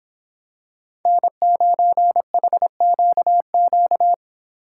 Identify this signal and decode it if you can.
Morse code, heard as a keyed tone, about 25 wpm, 700 Hz. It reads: N9HQQ